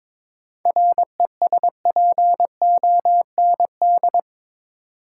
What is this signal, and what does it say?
Morse code, heard as a keyed tone, about 21 wpm, 700 Hz. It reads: RESPOND
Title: Seven Letter Words - 22wpm